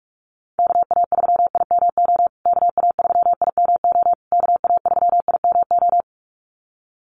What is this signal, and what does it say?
Morse code, heard as a keyed tone, about 44 wpm, 700 Hz. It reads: KA3IMO KA3IMO KA3IMO
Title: USA Call Signs Repeated 3x 45wpm